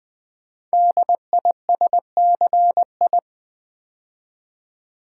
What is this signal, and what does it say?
Morse code, heard as a keyed tone, about 20 wpm, 700 Hz. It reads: DISCI
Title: Build A Word - 5 Letters At A Time - 10-Letter Words - 20wpm